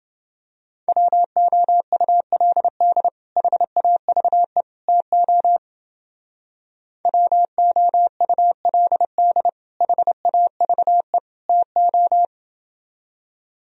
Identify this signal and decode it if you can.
Morse code, heard as a keyed tone, about 29 wpm, 700 Hz. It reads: WOULD HAVE TO WOULD HAVE TO